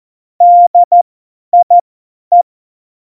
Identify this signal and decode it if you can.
Morse code, heard as a keyed tone, about 14 wpm, spaced as for 10 wpm, 700 Hz. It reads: DIE
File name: Morsekurs / Witze / Witz 27. Oktober 2020, 50 F70